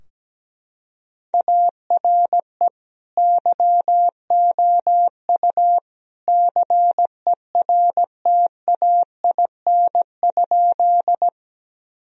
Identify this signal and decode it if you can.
Morse code, heard as a keyed tone, about 17 wpm, 700 Hz. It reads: ARE YOU CERTAIN?
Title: Sentences from Top 1000 Words - Review Part 2 - 17wpm